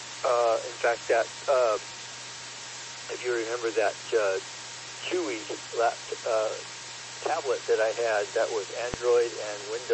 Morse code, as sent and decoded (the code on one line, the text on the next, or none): none